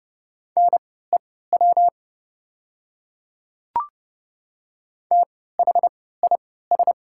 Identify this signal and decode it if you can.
Morse code, heard as a keyed tone, about 30 wpm, spaced as for 17 wpm, 700 Hz. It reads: NEW THIS
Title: 3x Character Spacing - Top 200 Words - 30wpm